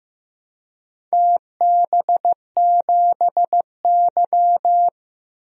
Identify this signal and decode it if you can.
Morse code, heard as a keyed tone, about 15 wpm, 700 Hz. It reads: TB7Y
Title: DXCC-100 Call Signs - 4 Characters - 15wpm